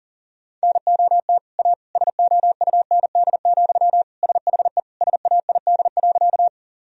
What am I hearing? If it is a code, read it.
Morse code, heard as a keyed tone, about 40 wpm, 700 Hz. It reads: NOT A SOUND, SHE SAID.
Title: Sentences from Top 1000 Words - Review Part 1 - 40wpm